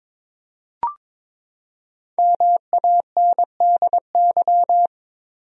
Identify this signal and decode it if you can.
Morse code, heard as a keyed tone, about 21 wpm, 700 Hz. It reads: MANDY
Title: USA Common Names - Max 5 Letters - 22wpm